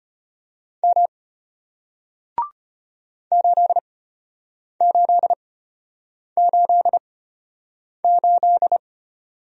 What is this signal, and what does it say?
Morse code, heard as a keyed tone, about 33 wpm, spaced as for 8 wpm, 700 Hz. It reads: M 8888